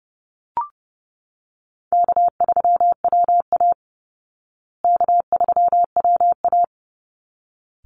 Morse code, heard as a keyed tone, about 30 wpm, 700 Hz. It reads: K3WA K3WA
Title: Call Signs Repeated 3x - 30wpm